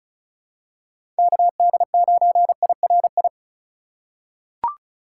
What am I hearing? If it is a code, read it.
Morse code, heard as a keyed tone, about 36 wpm, 700 Hz. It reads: KD9IRI
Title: USA Call Signs - 6 Characters - 2x Character Spacing - 35wpm